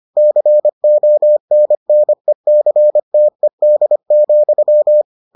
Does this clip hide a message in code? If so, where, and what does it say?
Yes: Morse code from 0.2 to 5.0 s, heard as a keyed tone, about 25 wpm, 600 Hz. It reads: CONNECTED,